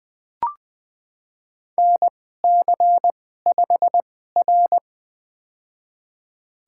Morse code, heard as a keyed tone, about 20 wpm, spaced as for 14 wpm, 700 Hz. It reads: NC5R
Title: USA Call Signs - 4 Characters - 2x Character Spacing - 20wpm